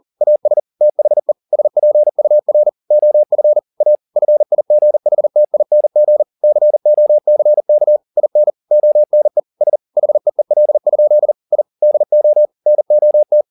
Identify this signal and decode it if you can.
Morse code, heard as a keyed tone, about 40 wpm, 600 Hz. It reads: AS THE SPUR OF A FIGHTING COCK IN ONE S HEEL? I DO NOT